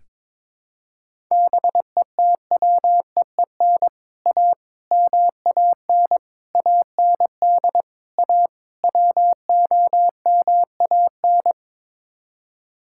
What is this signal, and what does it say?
Morse code, heard as a keyed tone, about 22 wpm, 700 Hz. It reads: BETWEEN A MAN AND A WOMAN
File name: practice / 22wpm / Sets of 6 Words - 22wpm